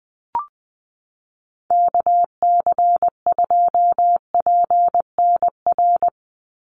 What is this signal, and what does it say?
Morse code, heard as a keyed tone, about 20 wpm, 700 Hz. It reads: KC2PNR
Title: USA Call Signs 20wpm